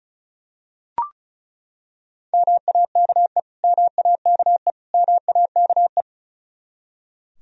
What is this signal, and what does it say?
Morse code, heard as a keyed tone, about 34 wpm, 700 Hz. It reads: MAKE MAKE MAKE
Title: Top 100 Words Repeated 35wpm